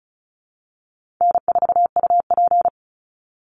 Morse code, heard as a keyed tone, about 34 wpm, 700 Hz. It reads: N4UP
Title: Call Signs - 35wpm